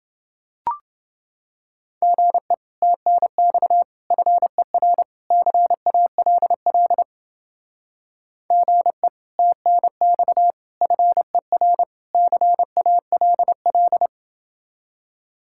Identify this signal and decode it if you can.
Morse code, heard as a keyed tone, about 28 wpm, 700 Hz. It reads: GE TNX FER CALL GE TNX FER CALL